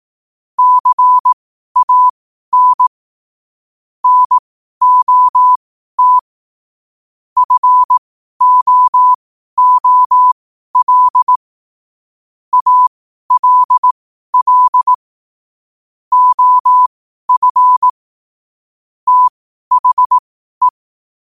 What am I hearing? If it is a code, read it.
Morse code, heard as a keyed tone, about 18 wpm, spaced as for 12 wpm, 1000 Hz. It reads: CAN NOT FOOL ALL OF THE
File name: qotdfiles / 12wpm